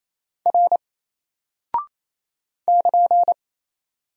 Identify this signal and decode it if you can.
Morse code, heard as a keyed tone, about 28 wpm, 700 Hz. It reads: R <KN>
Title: Single Character (Letter:Number:Punctuation:Prosign) 28wpm